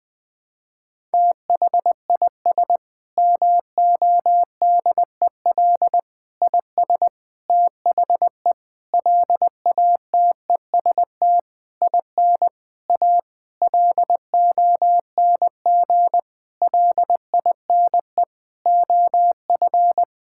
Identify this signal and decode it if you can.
Morse code, heard as a keyed tone, about 20 wpm, 700 Hz. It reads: THIS MODEL IS THE LATEST IN A LONG LINE OF
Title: Sentences with 8-Word Phrases - 20wpm